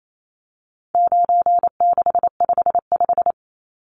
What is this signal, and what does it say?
Morse code, heard as a keyed tone, about 28 wpm, 700 Hz. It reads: 9655
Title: Numbers - 4-Digit 28wpm